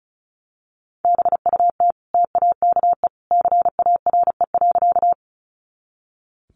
Morse code, heard as a keyed tone, about 34 wpm, 700 Hz. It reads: BUT TAKE CARE.
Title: Sentences from Top 300 Words 35wpm